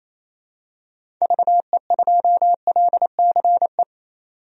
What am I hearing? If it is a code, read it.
Morse code, heard as a keyed tone, about 28 wpm, 700 Hz. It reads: VE2LCE
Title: DXCC-100 Call Signs - 6 Characters - 28wpm